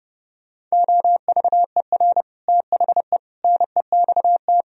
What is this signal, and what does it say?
Morse code, heard as a keyed tone, about 29 wpm, 700 Hz. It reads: OVER THE NEXT